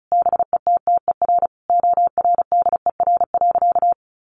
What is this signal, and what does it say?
Morse code, heard as a keyed tone, about 34 wpm, 700 Hz. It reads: BETTER ORDER.
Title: Sentences from Top 1500 Words 35wpm